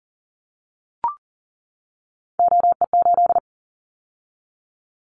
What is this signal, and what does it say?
Morse code, heard as a keyed tone, about 38 wpm, 700 Hz. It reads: OE8
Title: Callsign Prefixes for Top 50 DXCC - 40wpm